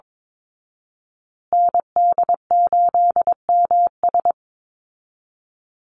Morse code, heard as a keyed tone, about 22 wpm, 700 Hz. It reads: ND8MS
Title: Call Signs Repeated 2x - 22wpm